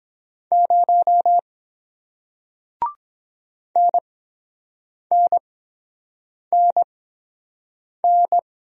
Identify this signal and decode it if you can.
Morse code, heard as a keyed tone, about 23 wpm, spaced as for 7 wpm, 700 Hz. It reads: 0 NNNN